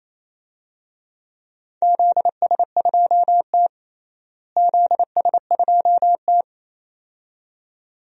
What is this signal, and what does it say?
Morse code, heard as a keyed tone, about 28 wpm, 700 Hz. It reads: ZS2T ZS2T
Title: DXCC-100 Call Signs - 4 Characters - Repeated 2x - 28wpm